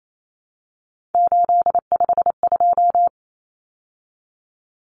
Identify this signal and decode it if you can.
Morse code, heard as a keyed tone, about 28 wpm, 700 Hz. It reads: 852
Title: Common Cut Numbers - 3-Digit 28wpm